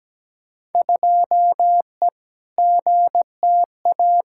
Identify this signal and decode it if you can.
Morse code, heard as a keyed tone, about 17 wpm, 700 Hz. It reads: 2E GTA
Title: Field Day Exchanges Encore - 17wpm